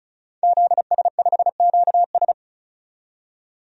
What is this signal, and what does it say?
Morse code, heard as a keyed tone, about 34 wpm, 700 Hz. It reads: ZS5QS
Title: DXCC-100 Call Signs - 5 Characters - 35wpm